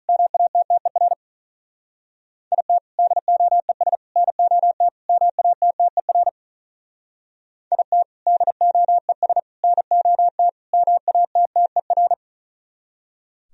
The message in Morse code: MATTER IT DOES NOT MATTER IT DOES NOT MATTER

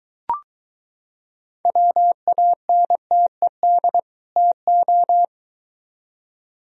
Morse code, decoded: WANTED TO

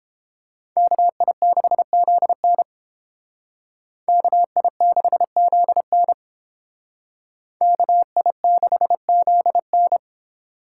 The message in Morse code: KI6ZN KI6ZN KI6ZN

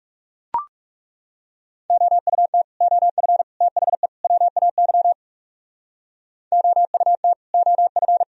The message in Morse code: OUT OF THE WAY OUT OF

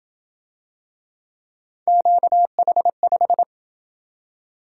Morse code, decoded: QH5

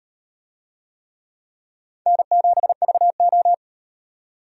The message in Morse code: N7VO

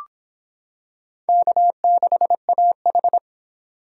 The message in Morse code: K6AH

